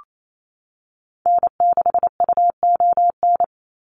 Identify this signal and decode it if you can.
Morse code, heard as a keyed tone, about 28 wpm, 700 Hz. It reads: N6UON
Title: USA Call Signs 28wpm